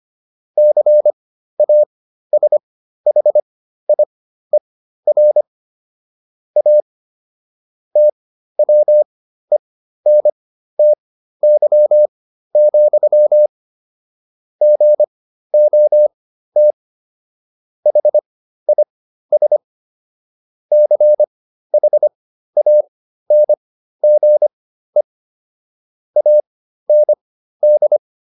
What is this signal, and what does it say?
Morse code, heard as a keyed tone, about 25 wpm, spaced as for 13 wpm, 600 Hz. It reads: CASHIER A TWENTY, GOT HIS CHANGE AND